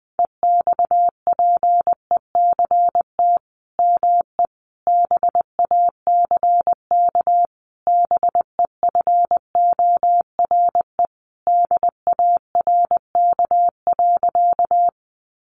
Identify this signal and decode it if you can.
Morse code, heard as a keyed tone, about 20 wpm, 700 Hz. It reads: EXPECT ME BACK BEFORE DARK.
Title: Sentences from Top 1100 Words 20wpm